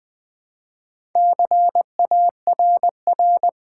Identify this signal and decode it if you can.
Morse code, heard as a keyed tone, about 20 wpm, 700 Hz. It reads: CARR
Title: Build A Word - 2 Letters At A Time - 6-Letter Words - 20wpm